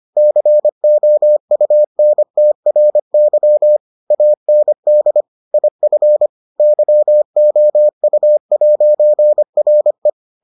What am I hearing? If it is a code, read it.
Morse code, heard as a keyed tone, about 25 wpm, 600 Hz. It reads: COUNTRY AND IF YOU'RE